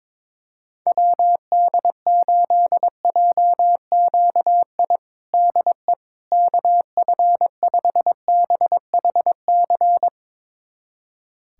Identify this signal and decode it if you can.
Morse code, heard as a keyed tone, about 21 wpm, 700 Hz. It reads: WD8JQI DE KF5BHC
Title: USA Callsign DE Callsign 22wpm